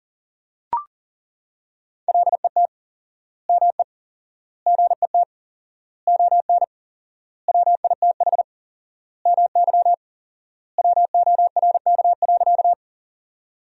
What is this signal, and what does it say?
Morse code, heard as a keyed tone, about 40 wpm, 700 Hz. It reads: LET ME GET ON WITH MY WORK.
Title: 4x Word-Spacing - Sentences from Top 300 Words - 40wpm